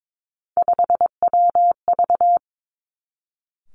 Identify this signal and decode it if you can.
Morse code, heard as a keyed tone, about 22 wpm, 700 Hz. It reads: 5WV
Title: Three Challenging Letter-Number Combinations 22wpm